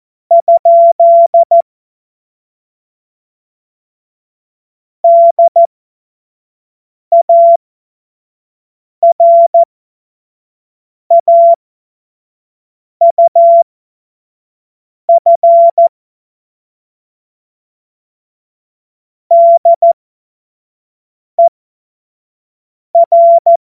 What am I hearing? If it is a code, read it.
Morse code, heard as a keyed tone, about 14 wpm, spaced as for 5 wpm, 680 Hz. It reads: ? DARAUF DER
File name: Morsekurs / Witze / Witz 02 April 2024 25 F70